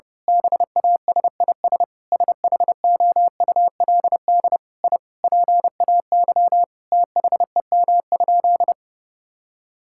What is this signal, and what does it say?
Morse code, heard as a keyed tone, about 32 wpm, 700 Hz. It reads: BASIS SHOULD I PAY THEM?
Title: Sentences from Top 1000 Words - Review Part 5 - 30wpm